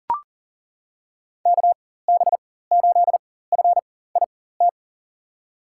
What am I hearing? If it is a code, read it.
Morse code, heard as a keyed tone, about 40 wpm, spaced as for 19 wpm, 700 Hz. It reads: KB8FIT
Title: USA Call Signs - 6 Characters - 4x Character Spacing - 40wpm